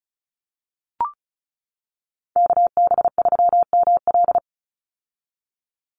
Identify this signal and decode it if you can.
Morse code, heard as a keyed tone, about 34 wpm, 700 Hz. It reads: KB3ML
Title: Call Signs - 35wpm